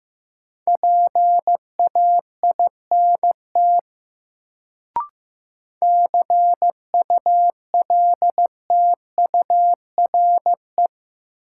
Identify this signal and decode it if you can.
Morse code, heard as a keyed tone, about 15 wpm, 700 Hz. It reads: PAINT CULTURE
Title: Sentences from Top 1000 Words - Review Part 6 - New Word Drill - 15wpm